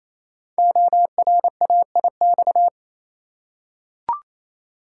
Morse code, heard as a keyed tone, about 28 wpm, 700 Hz. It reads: ORAIX
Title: Code Groups - 5 Letters - 28wpm